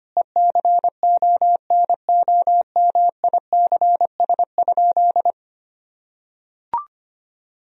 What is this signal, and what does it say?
Morse code, heard as a keyed tone, about 25 wpm, 700 Hz. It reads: ECONOMICS?
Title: Sentences from Top 1000 Words - Review Part 3 - 25wpm